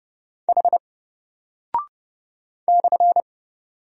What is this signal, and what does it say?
Morse code, heard as a keyed tone, about 30 wpm, 700 Hz. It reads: H /
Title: Single Character (Letter:Number:Punctuation:Prosign) Alt 30wpm